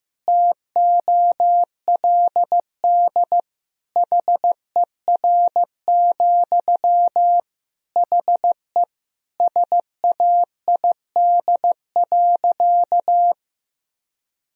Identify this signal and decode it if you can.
Morse code, heard as a keyed tone, about 16 wpm, 700 Hz. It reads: TOLD HER, HE SAID.
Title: Sentences from Top 1000 Words - Review Part 4 - 15wpm